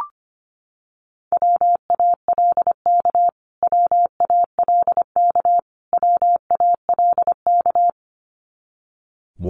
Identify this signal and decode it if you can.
Morse code, heard as a keyed tone, about 25 wpm, 700 Hz. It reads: WALK WALK WALK
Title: Top 500 Words Repeated 25wpm